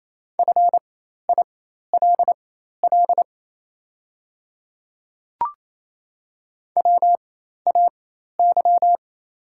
Morse code, heard as a keyed tone, about 27 wpm, spaced as for 13 wpm, 700 Hz. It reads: FILL WAY
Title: 4x Character Spacing - Top 500 Words - 28wpm